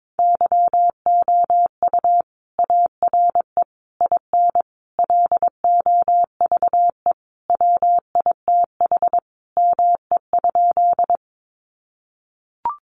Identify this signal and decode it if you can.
Morse code, heard as a keyed tone, about 22 wpm, 700 Hz. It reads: YOU ARE IN LOVE WITH ME?
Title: Sentences from Top 400 Words 22wpm